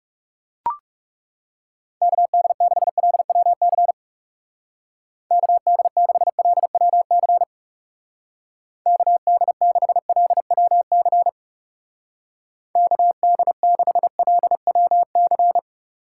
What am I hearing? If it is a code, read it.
Morse code, heard as a keyed tone, about 40 wpm, 700 Hz. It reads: KD6LWC KD6LWC KD6LWC KD6LWC